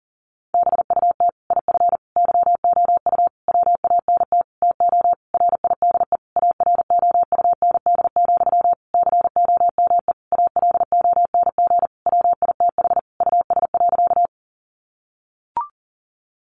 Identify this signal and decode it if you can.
Morse code, heard as a keyed tone, about 40 wpm, 700 Hz. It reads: BUT IF YOU WANT TO RIDE AROUND, COME ALONG WITH US.
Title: Sentences from Top 1200 Words 40wpm